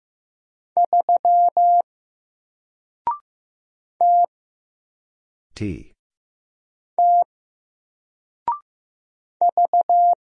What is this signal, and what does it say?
Morse code, heard as a keyed tone, about 15 wpm, 700 Hz. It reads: 3 T T V
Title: Single Character (Letter:Number:Punctuation:Prosign) 15wpm